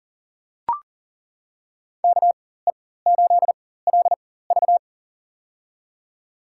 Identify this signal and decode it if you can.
Morse code, heard as a keyed tone, about 38 wpm, spaced as for 18 wpm, 700 Hz. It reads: KE8LV
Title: USA Call Signs - 4x Character Spacing - 40wpm